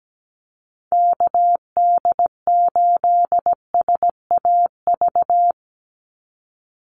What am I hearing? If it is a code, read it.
Morse code, heard as a keyed tone, about 17 wpm, 700 Hz. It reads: KD8SAV
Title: Call Signs - 17wpm